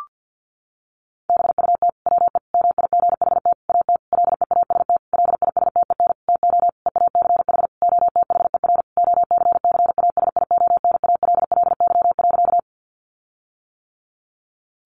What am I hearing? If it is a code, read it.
Morse code, heard as a keyed tone, about 50 wpm, 700 Hz. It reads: BUT WE MIGHT AT LEAST LISTEN TO EACH OTHER OCCASIONALLY.
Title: Sentences from Top 1300 Words 50wpm